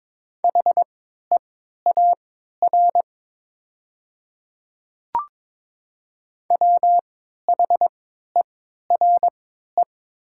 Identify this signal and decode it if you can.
Morse code, heard as a keyed tone, about 22 wpm, spaced as for 13 wpm, 700 Hz. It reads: HEAR WHERE
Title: 3x Character Spacing - Top 300 Words - 22wpm